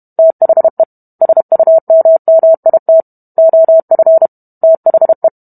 Morse code, heard as a keyed tone, about 32 wpm, 650 Hz. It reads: THE SUMMIT OF THE